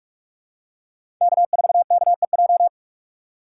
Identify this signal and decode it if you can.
Morse code, heard as a keyed tone, about 44 wpm, 700 Hz. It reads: K4KEJ
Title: USA Call Signs - 5 Characters - 6x Character Spacing - 45wpm